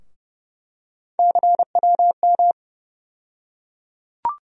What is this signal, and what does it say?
Morse code, heard as a keyed tone, about 30 wpm, 700 Hz. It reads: CWM
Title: USA Call Sign Postfixes - 3 Letters 30wpm